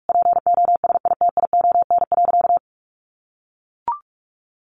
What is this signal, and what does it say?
Morse code, heard as a keyed tone, about 44 wpm, 700 Hz. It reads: POSITION.
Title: Sentences from Top 900 Words 45wpm